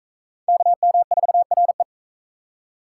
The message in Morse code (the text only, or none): KM4RE